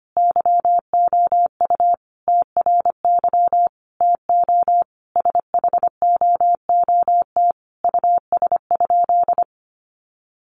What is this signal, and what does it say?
Morse code, heard as a keyed tone, about 25 wpm, 700 Hz. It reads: YOU TRY TO SHOOT US?
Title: Sentences from Top 1200 Words 25wpm